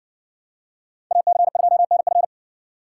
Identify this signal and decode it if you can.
Morse code, heard as a keyed tone, about 60 wpm, 700 Hz. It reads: AC2NF